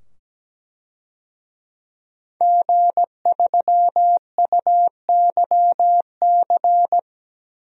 Morse code, heard as a keyed tone, about 17 wpm, 700 Hz. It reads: G3UYC
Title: DXCC-100 Call Signs - 5 Characters - 17wpm